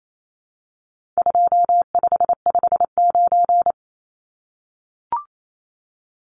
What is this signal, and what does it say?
Morse code, heard as a keyed tone, about 28 wpm, 700 Hz. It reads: 2559
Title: Numbers - 4-Digit 28wpm